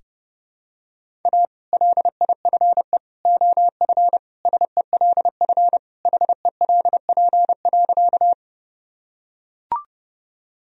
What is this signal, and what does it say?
Morse code, heard as a keyed tone, about 32 wpm, 700 Hz. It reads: A LIFE OF SELF HELP.
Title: Sentences from Top 1000 Words - Review Part 1 - 30wpm